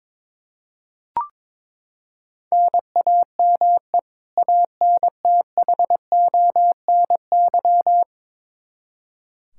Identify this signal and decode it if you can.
Morse code, heard as a keyed tone, about 21 wpm, 700 Hz. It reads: NAME ANTHONY
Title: K1USN SST QSO Practice 22wpm